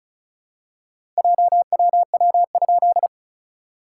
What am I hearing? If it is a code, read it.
Morse code, heard as a keyed tone, about 36 wpm, 690 Hz. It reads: JWW?
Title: SOTA QSO Practice 35wpm